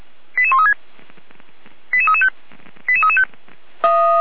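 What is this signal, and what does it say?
Five-tone selective call (ZVEI), heard as quick runs of five tones, tones of about 70 ms. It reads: 88136 88335 88335
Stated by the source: ZVEI five-tone selective call received on an SDR.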